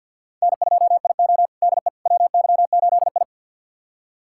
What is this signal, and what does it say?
Morse code, heard as a keyed tone, about 53 wpm, 700 Hz. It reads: N2IO DE WY8I
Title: USA Callsign DE Callsign - 4 Characters - 50wpm